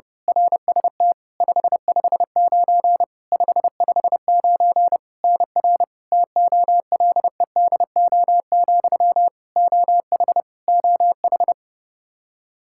Morse code, heard as a keyed tone, about 32 wpm, 700 Hz. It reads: RST 559 559 NR TOLEDO, OH OH
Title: Practice QSO - Exchange 1 - Beginning - 30wpm